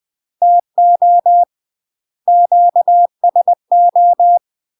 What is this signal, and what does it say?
Morse code, heard as a keyed tone, about 20 wpm, 700 Hz. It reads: TO QSO
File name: UK MASTER QSOs TRAINING / 20WPM / UK MASTER QSOs 0066